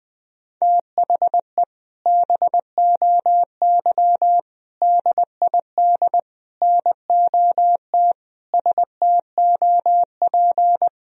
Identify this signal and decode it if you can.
Morse code, heard as a keyed tone, about 20 wpm, 700 Hz. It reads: THE BOY DID NOT STOP
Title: Sentences from Top 1000 Words - Review Part 2 - 20wpm